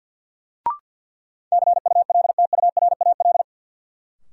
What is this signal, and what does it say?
Morse code, heard as a keyed tone, about 47 wpm, 700 Hz. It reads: CULTURAL
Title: Sentences from Top 1500 Words - New Word Drill Alt - 50wpm